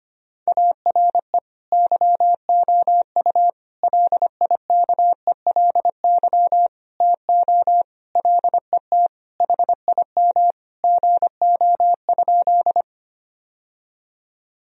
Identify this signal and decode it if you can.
Morse code, heard as a keyed tone, about 25 wpm, 700 Hz. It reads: ARE YOU LIKELY TO LET HIM GO?
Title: Sentences from Top 1000 Words - Review Part 4 - 25wpm